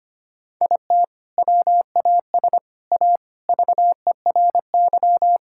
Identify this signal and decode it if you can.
Morse code, heard as a keyed tone, about 25 wpm, 700 Hz. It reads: IT WAS A VERY
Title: Sentences from Top 1000 Words - Review Part 6 - 25wpm